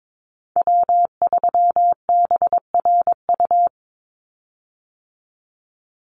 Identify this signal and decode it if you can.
Morse code, heard as a keyed tone, about 22 wpm, 700 Hz. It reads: W3BRU
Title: USA Call Signs 22wpm